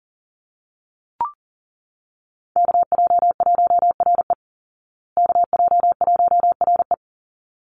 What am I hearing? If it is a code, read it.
Morse code, heard as a keyed tone, about 40 wpm, 700 Hz. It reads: KJ1RE KJ1RE
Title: Call Signs Repeated 3x - 40wpm